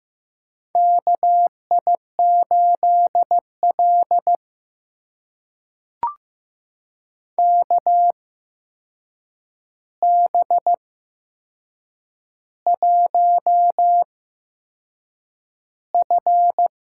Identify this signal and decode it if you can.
Morse code, heard as a keyed tone, about 15 wpm, spaced as for 6 wpm, 700 Hz. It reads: KI8L KB1F
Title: USA Call Signs - 4 Characters - 8x Character Spacing - 15wpm